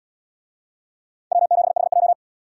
Morse code, heard as a keyed tone, about 74 wpm, 700 Hz. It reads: W7HJ